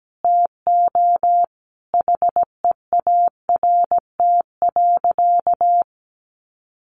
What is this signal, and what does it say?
Morse code, heard as a keyed tone, about 17 wpm, 700 Hz. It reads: TO HEART.
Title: Sentences from Top 1400 Words 17wpm